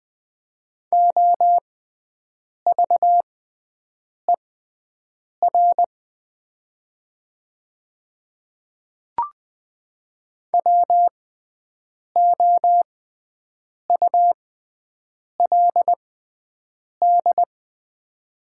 Morse code, heard as a keyed tone, about 20 wpm, spaced as for 7 wpm, 700 Hz. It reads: OVER WOULD
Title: 6x Character Spacing - Top 100 Words - 20wpm